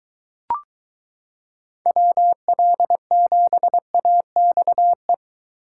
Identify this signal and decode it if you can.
Morse code, heard as a keyed tone, about 23 wpm, 700 Hz. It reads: WL7AXE